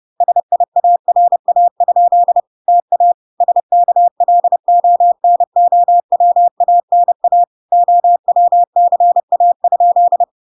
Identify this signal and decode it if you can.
Morse code, heard as a keyed tone, about 30 wpm, 700 Hz. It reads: SIARA? TA SKLONOWANA OWCA?